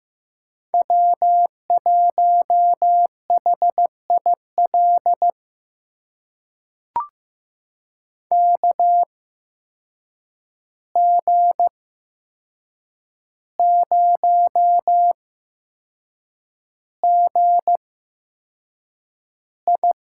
Morse code, heard as a keyed tone, about 15 wpm, spaced as for 6 wpm, 700 Hz. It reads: W1HIL KG0GI
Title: USA Call Signs - 5 Characters - 8x Character Spacing - 15wpm